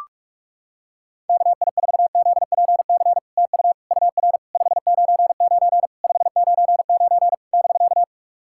Morse code, heard as a keyed tone, about 41 wpm, 700 Hz. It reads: KI4ZPC TU UR 599 599 <BK>
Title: SOTA QSO Practice 45wpm